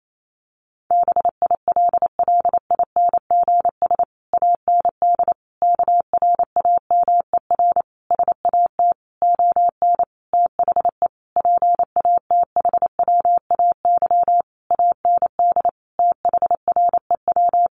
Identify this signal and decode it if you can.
Morse code, heard as a keyed tone, about 28 wpm, 700 Hz. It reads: BILLINGS AND KRAMER SAT ON THE PATHWAY AND THREW
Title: A Sound Of Thunder 28wpm